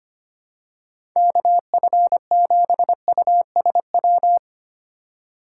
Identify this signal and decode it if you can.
Morse code, heard as a keyed tone, about 25 wpm, 700 Hz. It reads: KF7USW
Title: USA Call Signs - 6 Characters - 8x Character Spacing - 25wpm